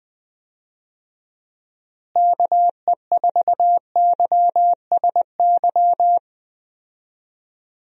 Morse code, heard as a keyed tone, about 20 wpm, 700 Hz. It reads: KE4YSY
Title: USA Call Signs - 6 Characters - Repeated 2x - 20wpm